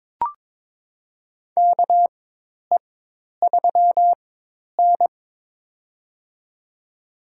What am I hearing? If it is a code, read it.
Morse code, heard as a keyed tone, about 22 wpm, spaced as for 10 wpm, 700 Hz. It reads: KE3N
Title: USA Call Signs - 4 Characters - 4x Character Spacing - 22wpm